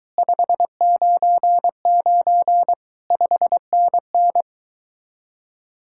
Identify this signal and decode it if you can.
Morse code, heard as a keyed tone, about 23 wpm, 700 Hz. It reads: 599 5NN